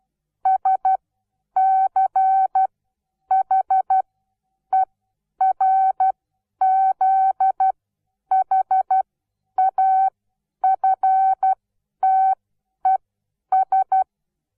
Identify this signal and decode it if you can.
Morse code, heard as a keyed tone, about 12 wpm, spaced as for 9 wpm, 760 Hz. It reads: SCHERZHAFTES
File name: Onlinelehrgang / morsen / DJ4UF Morsen17